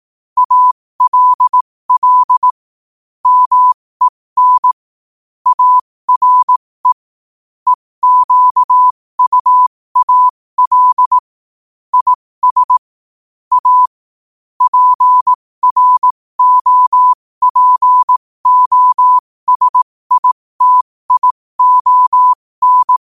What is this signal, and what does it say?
Morse code, heard as a keyed tone, about 18 wpm, spaced as for 15 wpm, 1000 Hz. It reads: ALL MEN ARE EQUAL IS A PROPOSITION
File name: qotdfiles / 15wpm